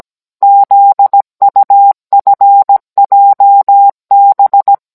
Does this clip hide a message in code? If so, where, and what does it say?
Yes: Morse code from 0.4 to 4.7 s, heard as a keyed tone, about 17 wpm, 800 Hz. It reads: ZUFJB